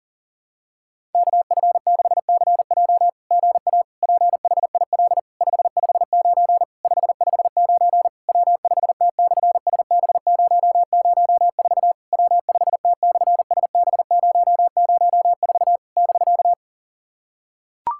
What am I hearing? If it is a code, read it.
Morse code, heard as a keyed tone, about 40 wpm, 690 Hz. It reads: KF6CJ GA PHIL 559 559 W5T/SB004 W5T/SB004 <BK>
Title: SOTA QSO Practice 40wpm